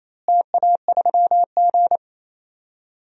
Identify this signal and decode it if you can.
Morse code, heard as a keyed tone, about 28 wpm, 700 Hz. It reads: TA3G
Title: DXCC-100 Call Signs - 4 Characters - Repeated 2x - 28wpm